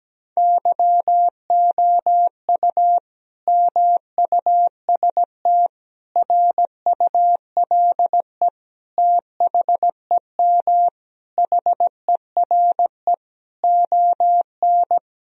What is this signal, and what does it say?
Morse code, heard as a keyed tone, about 17 wpm, 700 Hz. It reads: YOU MUST RULE THEM HERE ON
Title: Sentences from Top 1000 Words - Review Part 2 - 17wpm